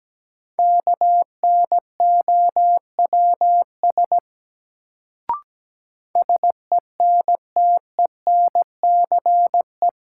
Morse code, heard as a keyed tone, about 17 wpm, 700 Hz. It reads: KNOWS SENTENCE
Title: Sentences from Top 1000 Words - Review Part 1 - New Word Drill - 17wpm